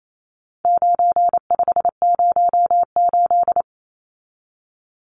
Morse code, heard as a keyed tone, about 28 wpm, 700 Hz. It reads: 9508
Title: Numbers - 4-Digit 28wpm